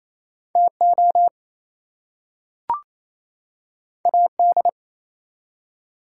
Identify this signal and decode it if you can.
Morse code, heard as a keyed tone, about 28 wpm, 700 Hz. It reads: TO AD
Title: Two Letter Words - 28wpm